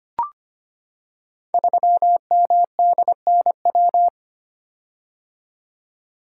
Morse code, heard as a keyed tone, about 25 wpm, 700 Hz. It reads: 3MDNW
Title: Code Groups - 5 Letter:Numbers - 25wpm